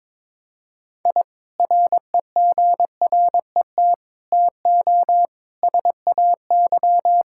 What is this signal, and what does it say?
Morse code, heard as a keyed tone, about 22 wpm, 700 Hz. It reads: I REGRET TO SAY
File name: practice / 22wpm / Sentences with 3-Word Phrases - 22wpm